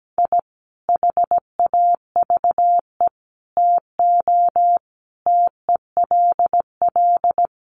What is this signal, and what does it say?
Morse code, heard as a keyed tone, about 17 wpm, 700 Hz. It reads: I HAVE TO TELL
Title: Sets of 5 Words 17wpm